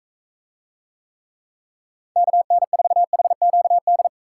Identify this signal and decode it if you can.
Morse code, heard as a keyed tone, about 44 wpm, 700 Hz. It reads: KN4HQD